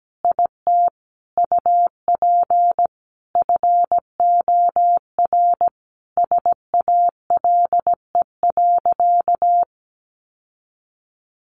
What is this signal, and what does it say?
Morse code, heard as a keyed tone, about 17 wpm, 700 Hz. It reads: IT UP FOR SALE.
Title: Sentences from Top 1400 Words 17wpm